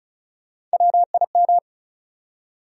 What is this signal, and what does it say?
Morse code, heard as a keyed tone, about 34 wpm, 700 Hz. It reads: WIM
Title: USA Call Sign Postfixes - 3 Letters 35wpm